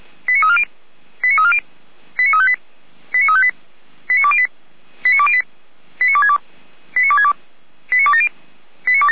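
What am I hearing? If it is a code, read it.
Five-tone selective call (ZVEI), heard as quick runs of five tones, tones of about 70 ms. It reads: 89340 89340 89358 89358 89208 89208 89262 89262 89270
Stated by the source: ZVEI five-tone call 89303 received by SDR, channel 455.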